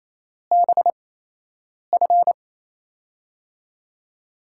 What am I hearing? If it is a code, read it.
Morse code, heard as a keyed tone, about 28 wpm, spaced as for 8 wpm, 700 Hz. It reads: BF
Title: USA Call Signs - 6 Characters - 8x Character Spacing - 28wpm